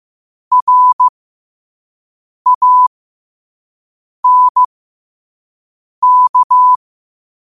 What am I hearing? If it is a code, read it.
Morse code, heard as a keyed tone, about 15 wpm, spaced as for 5 wpm, 1000 Hz. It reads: RANK
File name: qotdfiles / 5wpm